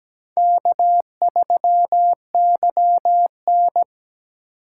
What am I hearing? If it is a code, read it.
Morse code, heard as a keyed tone, about 17 wpm, 700 Hz. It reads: K3YN
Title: USA Call Signs - 4x Character Spacing - 17wpm